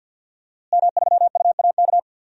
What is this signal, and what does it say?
Morse code, heard as a keyed tone, about 53 wpm, 700 Hz. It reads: M3UAK
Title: DXCC-100 Call Signs - 5 Characters - Repeated 2x - 50wpm